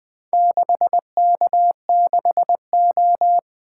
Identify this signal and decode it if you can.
Morse code, heard as a keyed tone, about 20 wpm, 700 Hz. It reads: 6K6O
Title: Four Letter-Number Combinations - 20wpm